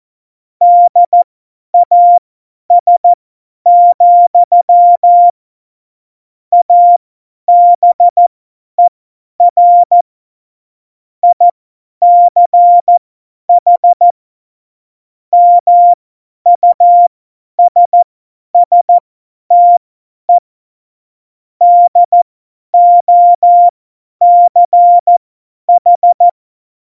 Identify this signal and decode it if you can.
Morse code, heard as a keyed tone, about 14 wpm, spaced as for 10 wpm, 700 Hz. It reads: DAS, ABER ICH MUSSTE DOCH